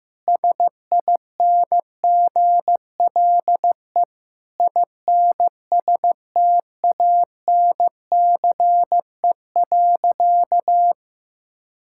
Morse code, heard as a keyed tone, about 15 wpm, 700 Hz. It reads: SINGLE INSTANCE.
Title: Sentences from Top 1000 Words - Review Part 5 - 15wpm